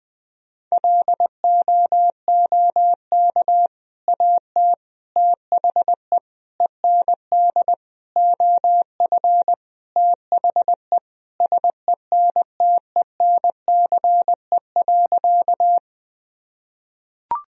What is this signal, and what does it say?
Morse code, heard as a keyed tone, about 20 wpm, 700 Hz. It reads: LOOK AT THE END OF THE SENTENCE.
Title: Sentences from Top 1000 Words - Review Part 1 - 20wpm